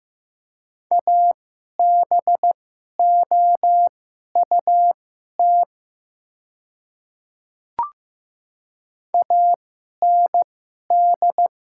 Morse code, heard as a keyed tone, about 15 wpm, spaced as for 11 wpm, 700 Hz. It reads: ABOUT AND
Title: 2x Character Spacing - Top 100 Words - 15wpm